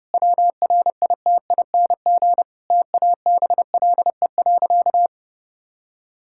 Morse code, heard as a keyed tone, about 30 wpm, 700 Hz. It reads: WRITING TABLE.
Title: Sentences from Top 1000 Words - Review Part 5 - 30wpm